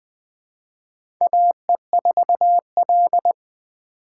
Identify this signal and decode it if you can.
Morse code, heard as a keyed tone, about 20 wpm, 700 Hz. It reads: AE4L
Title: USA Call Signs - 3x Character Spacing - 20wpm